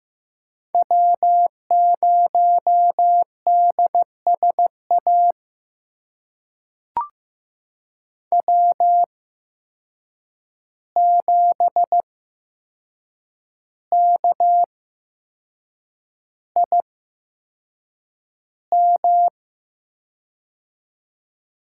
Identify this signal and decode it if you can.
Morse code, heard as a keyed tone, about 15 wpm, spaced as for 6 wpm, 700 Hz. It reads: W0DSA W7KIM
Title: USA Call Signs - 5 Characters - 8x Character Spacing - 15wpm